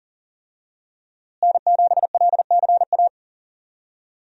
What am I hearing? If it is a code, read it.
Morse code, heard as a keyed tone, about 39 wpm, 700 Hz. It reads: N7LCA
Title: USA Call Signs - 5 Characters - Repeated 2x - 40wpm